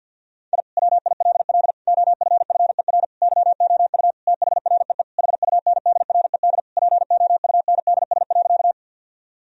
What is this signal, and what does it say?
Morse code, heard as a keyed tone, about 50 wpm, 700 Hz. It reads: I WILL OFFER YOU THREE HUNDRED POUNDS.
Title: Sentences from Top 1000 Words - Review Part 4 - 50wpm